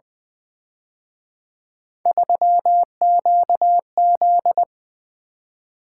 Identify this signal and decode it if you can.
Morse code, heard as a keyed tone, about 23 wpm, 700 Hz. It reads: 3QZ
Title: Speed-Racing - Three Letter-Number Combinations - 20wpm